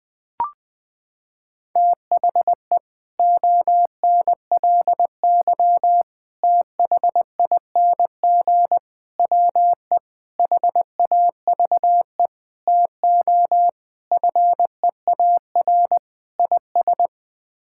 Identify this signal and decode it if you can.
Morse code, heard as a keyed tone, about 20 wpm, 700 Hz. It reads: THE ONLY THING WE HAVE TO FEAR IS
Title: Sets of 8 Words - 20wpm